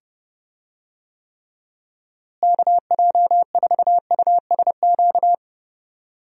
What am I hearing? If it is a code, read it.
Morse code, heard as a keyed tone, about 31 wpm, 700 Hz. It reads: KJ4USQ
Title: USA Call Signs - 6 Characters - Repeated 2x - 30wpm